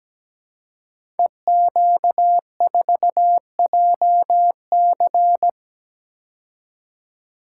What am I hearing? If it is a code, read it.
Morse code, heard as a keyed tone, about 17 wpm, 700 Hz. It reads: EQ4JC
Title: Code Groups - 5 Letter:Numbers - 17wpm